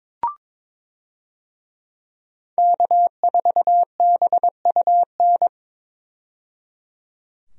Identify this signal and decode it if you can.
Morse code, heard as a keyed tone, about 22 wpm, 700 Hz. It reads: K4BUN
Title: USA Call Signs - 5 Characters - 22wpm